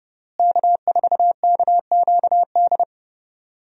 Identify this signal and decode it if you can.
Morse code, heard as a keyed tone, about 28 wpm, 700 Hz. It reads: K4KQD
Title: USA Call Signs - 5 Characters - Repeated 3x - 30wpm